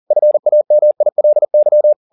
Morse code, heard as a keyed tone, about 40 wpm, 600 Hz. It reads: FAMILY
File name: the old man and the sea 2 40wpm